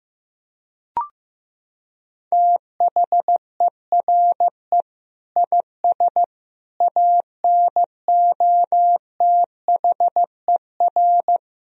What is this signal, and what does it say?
Morse code, heard as a keyed tone, about 15 wpm, 700 Hz. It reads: THERE IS ANOTHER
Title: Sentences from Top 1000 Words - Review Part 4 - 15wpm